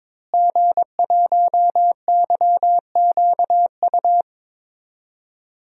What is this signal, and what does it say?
Morse code, heard as a keyed tone, about 22 wpm, 700 Hz. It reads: G1YQU
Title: DXCC-100 Call Signs - 5 Characters - Repeated 2x - 22wpm